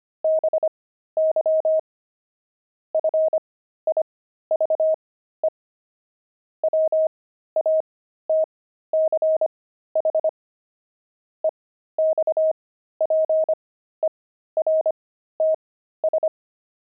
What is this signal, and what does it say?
Morse code, heard as a keyed tone, about 25 wpm, spaced as for 13 wpm, 630 Hz. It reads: BY FIVE WATCH EXPERTS
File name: pangram-fs-13-cs-25-watches